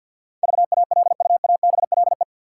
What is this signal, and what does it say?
Morse code, heard as a keyed tone, about 50 wpm, 700 Hz. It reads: VALUABLE
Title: Sentences from Top 1100 Words - New Word Drill - 50wpm